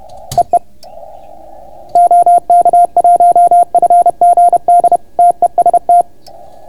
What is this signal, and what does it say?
Morse code, heard as a keyed tone, about 31 wpm, 690 Hz. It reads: EE OK1FGD TEST